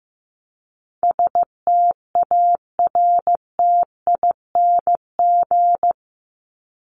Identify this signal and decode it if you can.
Morse code, heard as a keyed tone, about 15 wpm, 700 Hz. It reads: STARTING
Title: ING Suffix - Top 100 Words with Base Words 15wpm